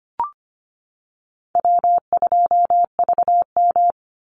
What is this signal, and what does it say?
Morse code, heard as a keyed tone, about 25 wpm, 700 Hz. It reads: W2VM
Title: Easy Call Signs - 25wpm